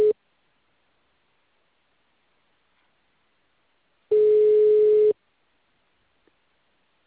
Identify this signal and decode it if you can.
Ringback tone (Europe, 425 Hz).